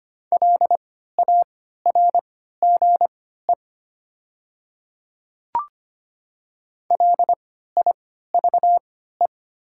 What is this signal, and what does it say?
Morse code, heard as a keyed tone, about 25 wpm, spaced as for 14 wpm, 700 Hz. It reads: LARGE LIVE
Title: 3x Character Spacing - Top 300 Words - 25wpm